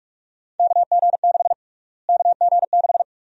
Morse code, heard as a keyed tone, about 43 wpm, 700 Hz. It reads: KG6 KG6
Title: USA Call Sign Prefixes - Repeated 2x - 45wpm